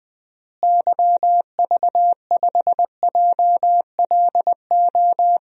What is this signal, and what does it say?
Morse code, heard as a keyed tone, about 20 wpm, 700 Hz. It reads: YV5JLO
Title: DXCC-100 Call Signs - 6 Characters - Repeated 3x - 20wpm